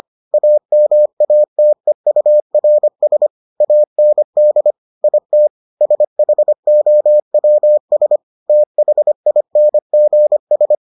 Morse code, heard as a keyed tone, about 25 wpm, 600 Hz. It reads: AMATEURS AND IT SHOWS THINGS